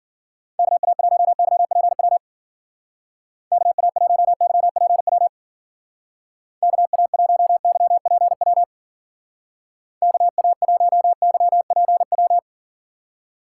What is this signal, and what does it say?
Morse code, heard as a keyed tone, about 51 wpm, 700 Hz. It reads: KA1YPW KA1YPW KA1YPW KA1YPW